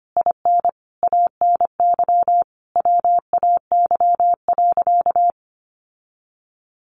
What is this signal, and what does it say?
Morse code, heard as a keyed tone, about 25 wpm, 700 Hz. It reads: IN ANY WAY.
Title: Sentences from Top 1100 Words 25wpm